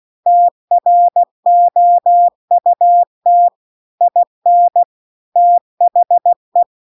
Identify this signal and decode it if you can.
Morse code, heard as a keyed tone, about 16 wpm, 700 Hz. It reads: TROUT IN THE